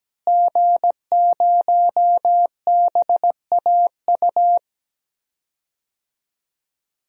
Morse code, heard as a keyed tone, about 17 wpm, 700 Hz. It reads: G0BAU
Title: DXCC-100 Call Signs - 5 Characters - Repeated 3x - 17wpm